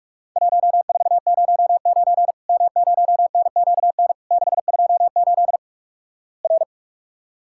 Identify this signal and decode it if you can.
Morse code, heard as a keyed tone, about 46 wpm, 700 Hz. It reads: 1409 M0NQN 628 R
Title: Worked All Europe DX Contest - QTC Exchange - 45wpm